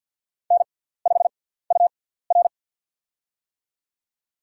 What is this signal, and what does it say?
Morse code, heard as a keyed tone, about 54 wpm, spaced as for 17 wpm, 700 Hz. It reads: N5UR
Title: USA Call Signs - 4 Characters - 6x Character Spacing - 50wpm